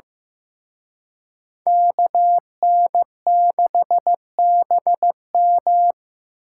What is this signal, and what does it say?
Morse code, heard as a keyed tone, about 18 wpm, 700 Hz. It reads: KN6BM
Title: Speed Racing - USA 2x2 Call Signs 15wpm